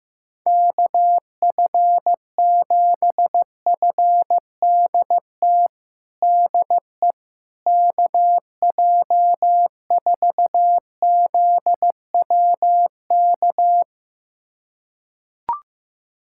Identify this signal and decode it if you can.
Morse code, heard as a keyed tone, about 15 wpm, 700 Hz. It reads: KF7FDT DE KJ4ZWK
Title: USA Callsign DE Callsign - 6 Characters - 15wpm